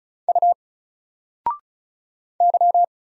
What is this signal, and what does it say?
Morse code, heard as a keyed tone, about 36 wpm, 700 Hz. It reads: U Y
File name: Single-Letter-Number-35wpm